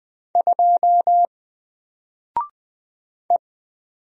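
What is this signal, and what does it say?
Morse code, heard as a keyed tone, about 20 wpm, 700 Hz. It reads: 2 E
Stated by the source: Single letter and number Morse drill at 20 wpm, from N4LCD.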